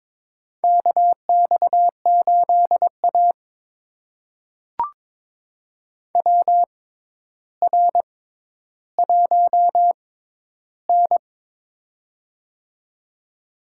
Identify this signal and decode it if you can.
Morse code, heard as a keyed tone, about 21 wpm, spaced as for 11 wpm, 700 Hz. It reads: KX8A WR1N
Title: USA Call Signs - 4 Characters - 6x Character Spacing - 22wpm